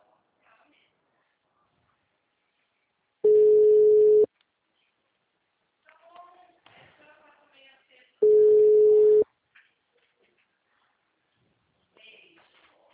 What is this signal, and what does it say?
Ringback tone (Europe, 425 Hz), with speech.
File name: out- -2016- - - .30297